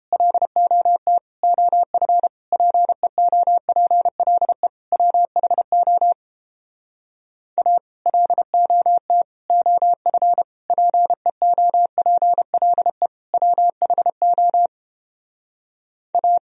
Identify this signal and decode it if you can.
Morse code, heard as a keyed tone, about 31 wpm, 700 Hz. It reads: LOT OF PEOPLE WHO A LOT OF PEOPLE WHO A